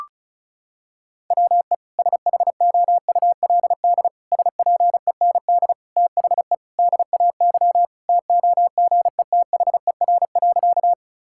Morse code, heard as a keyed tone, about 34 wpm, 700 Hz. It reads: WE SHOULD SPEND THE DAY TOGETHER.
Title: Sentences from Top 1000 Words - Review Part 5 - 35wpm